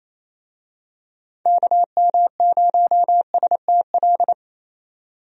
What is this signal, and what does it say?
Morse code, heard as a keyed tone, about 28 wpm, 700 Hz. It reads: KM0STL
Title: USA Call Signs - 6 Characters - 28wpm